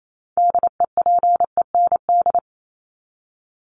Morse code, heard as a keyed tone, about 28 wpm, 700 Hz. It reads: DEPEND
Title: Top 1001-1250 Words 28wpm